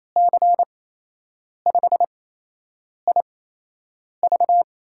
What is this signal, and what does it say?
Morse code, heard as a keyed tone, about 28 wpm, spaced as for 8 wpm, 700 Hz. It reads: C5IV
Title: USA Call Signs - 5 Characters - 8x Character Spacing - 28wpm